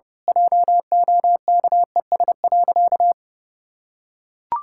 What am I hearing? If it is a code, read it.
Morse code, heard as a keyed tone, about 30 wpm, 700 Hz. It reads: JOKES.
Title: Pangram Challenge v2 - 30wpm